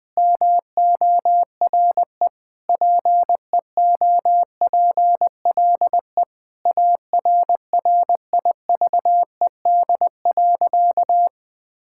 Morse code, heard as a keyed tone, about 20 wpm, 700 Hz. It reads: MORE PEOPLE ARRIVED.
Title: Sentences with 3-Word Phrases - Spoken Beginning - 20wpm